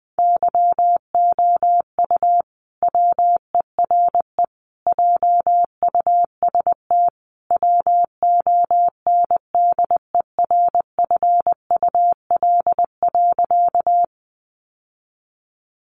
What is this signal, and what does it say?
Morse code, heard as a keyed tone, about 20 wpm, 700 Hz. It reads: YOU WERE JUST WONDERFUL.
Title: Sentences from Top 1100 Words 20wpm